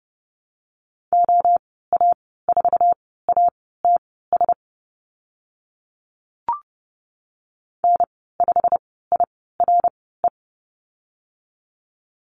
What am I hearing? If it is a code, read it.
Morse code, heard as a keyed tone, about 30 wpm, spaced as for 17 wpm, 690 Hz. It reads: OA4ATS N5IRE
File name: Lesson 15g - Number 5 - Call Signs 30wpm